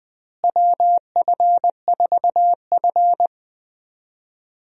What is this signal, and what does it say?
Morse code, heard as a keyed tone, about 21 wpm, 700 Hz. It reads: WF4F
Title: USA Call Signs - 4 Characters - Repeated 3x - 20wpm